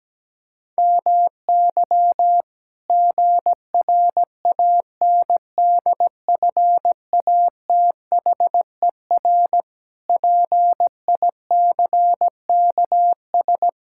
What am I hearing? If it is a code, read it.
Morse code, heard as a keyed tone, about 17 wpm, 700 Hz. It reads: MY GRANDFATHER PICKS
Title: Pangram Challenge v2 - 17wpm